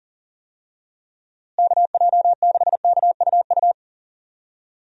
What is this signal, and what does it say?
Morse code, heard as a keyed tone, about 40 wpm, 700 Hz. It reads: KJ6KUU
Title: USA Call Signs - 6 Characters - Repeated 3x - 40wpm